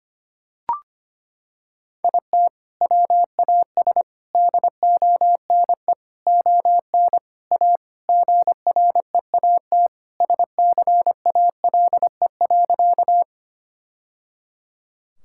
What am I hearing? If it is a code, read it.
Morse code, heard as a keyed tone, about 25 wpm, 700 Hz. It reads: IT WAS DONE ON A GREAT SCALE.
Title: Sentences from Top 1000 Words - Review Part 5 - 25wpm